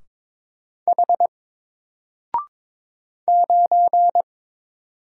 Morse code, heard as a keyed tone, about 21 wpm, 700 Hz. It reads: H 9
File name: Single-Letter-Number-22wpm